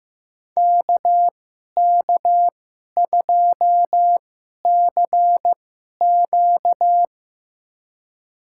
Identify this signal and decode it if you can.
Morse code, heard as a keyed tone, about 15 wpm, spaced as for 11 wpm, 700 Hz. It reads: KK2CQ
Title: USA Call Signs - 5 Characters - 2x Character Spacing - 15wpm